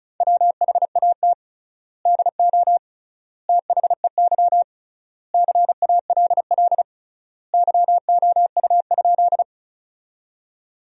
Morse code, heard as a keyed tone, about 36 wpm, 700 Hz. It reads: WHAT DO THEY CALL YOU?
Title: 3x Word-Spacing - Sentences from Top 100 Words - 35wpm